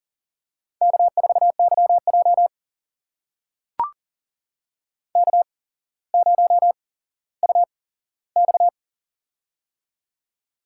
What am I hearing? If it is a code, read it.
Morse code, heard as a keyed tone, about 38 wpm, spaced as for 17 wpm, 700 Hz. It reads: K4YJ K0UX
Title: USA Call Signs - 4 Characters - 8x Character Spacing - 40wpm